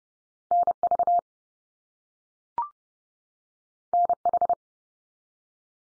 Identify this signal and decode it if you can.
Morse code, heard as a keyed tone, about 30 wpm, 700 Hz. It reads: NV NH
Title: US States 30WPM